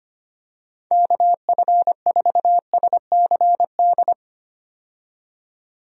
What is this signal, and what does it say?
Morse code, heard as a keyed tone, about 25 wpm, 700 Hz. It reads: KF4SCD
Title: USA Call Signs - 6 Characters - 25wpm